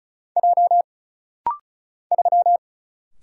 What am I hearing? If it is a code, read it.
Morse code, heard as a keyed tone, about 36 wpm, 700 Hz. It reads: J 3
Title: Single Letter-Number Alt - Rapid-Fire 35wpm